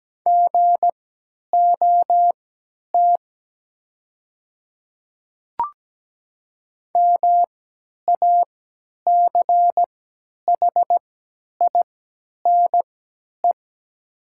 Morse code, heard as a keyed tone, about 17 wpm, spaced as for 10 wpm, 700 Hz. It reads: GOT MACHINE
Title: 3x Character Spacing - Top 500 Words - 17wpm